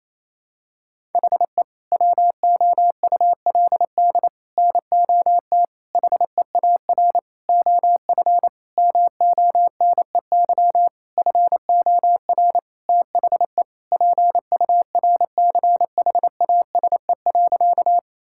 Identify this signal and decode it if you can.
Morse code, heard as a keyed tone, about 28 wpm, 700 Hz. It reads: HE WOULD NOT HEAR OF MONEY FOR THE PURCHASE.
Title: Sentences from Top 1000 Words - Review Part 5 - 28wpm